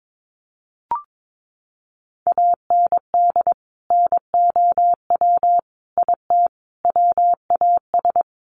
Morse code, heard as a keyed tone, about 22 wpm, 700 Hz. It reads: AND NOW IT WAS
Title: Sentences from Top 1100 Words 22wpm